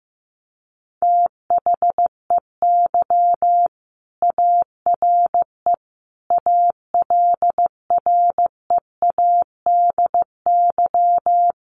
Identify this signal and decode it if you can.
Morse code, heard as a keyed tone, about 15 wpm, 700 Hz. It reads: THEY ARE ALREADY
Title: Sentences from Top 1500 Words 15wpm